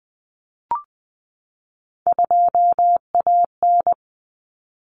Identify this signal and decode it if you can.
Morse code, heard as a keyed tone, about 20 wpm, 700 Hz. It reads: 2AN
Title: Common Cut Numbers - 3-Digit 20wpm